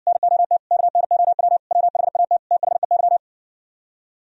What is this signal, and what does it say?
Morse code, heard as a keyed tone, about 59 wpm, 700 Hz. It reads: NOT KNOW WHAT THEY